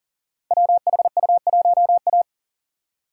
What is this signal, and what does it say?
Morse code, heard as a keyed tone, about 39 wpm, 690 Hz. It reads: WHU1A
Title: Code Groups - 5 Letter:Numbers - 40wpm